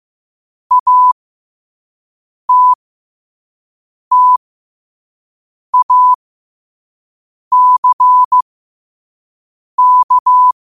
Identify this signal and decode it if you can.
Morse code, heard as a keyed tone, about 15 wpm, spaced as for 5 wpm, 1000 Hz. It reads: ATTACK
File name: qotdfiles / 5wpm